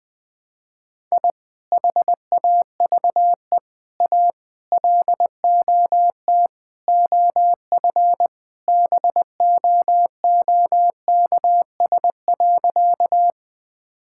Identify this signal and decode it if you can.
Morse code, heard as a keyed tone, about 21 wpm, 700 Hz. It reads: I HAVE A LOT OF BOOKS.
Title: Sentences with 5-Word Phrases - 20wpm